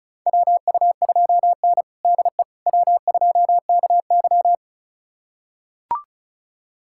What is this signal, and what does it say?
Morse code, heard as a keyed tone, about 36 wpm, 700 Hz. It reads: WU2N DE W2KY
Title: USA Callsign DE Callsign - 4 Characters - 35wpm